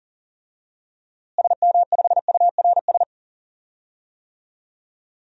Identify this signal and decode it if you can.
Morse code, heard as a keyed tone, about 40 wpm, 700 Hz. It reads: SM5URS
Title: DXCC-100 Call Signs - 6 Characters - Repeated 3x - 40wpm